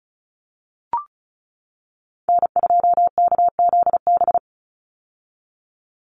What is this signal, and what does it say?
Morse code, heard as a keyed tone, about 34 wpm, 700 Hz. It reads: N2KZB
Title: USA Call Signs 35wpm